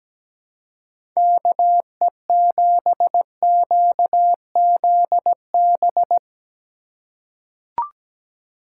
Morse code, heard as a keyed tone, about 17 wpm, 700 Hz. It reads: KE7QZB
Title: USA Call Signs - 3x Character Spacing - 17wpm